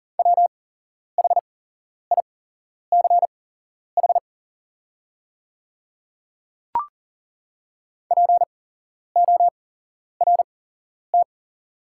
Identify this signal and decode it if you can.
Morse code, heard as a keyed tone, about 40 wpm, spaced as for 11 wpm, 700 Hz. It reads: WHICH PORT
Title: 8x Character Spacing - Top 300 Words - 40wpm